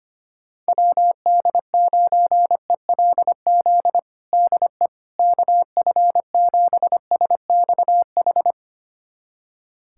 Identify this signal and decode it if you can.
Morse code, heard as a keyed tone, about 25 wpm, 700 Hz. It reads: WD9ELZ DE KF7SXH
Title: USA Callsign DE Callsign - 6 Characters - 25wpm